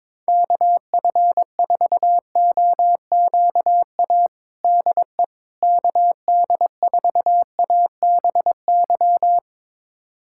Morse code, heard as a keyed tone, about 21 wpm, 700 Hz. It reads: KF4OQA DE KD4ABY
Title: USA Callsign DE Callsign 22wpm